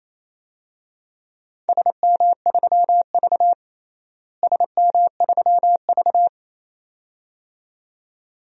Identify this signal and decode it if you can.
Morse code, heard as a keyed tone, about 28 wpm, 700 Hz. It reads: SM3V SM3V
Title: DXCC-100 Call Signs - 4 Characters - Repeated 2x - 28wpm